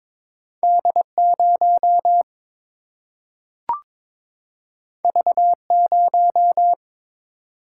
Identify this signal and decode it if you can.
Morse code, heard as a keyed tone, about 22 wpm, 700 Hz. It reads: D 0 V 0
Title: Single-Letter Single-Number 22wpm